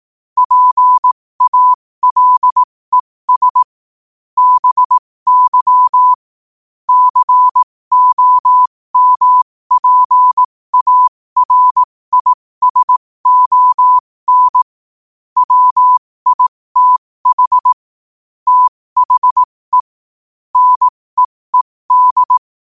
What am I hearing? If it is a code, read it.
Morse code, heard as a keyed tone, about 18 wpm, spaced as for 15 wpm, 1000 Hz. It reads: PALES BY COMPARISON WITH THE NEED
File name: qotdfiles / 15wpm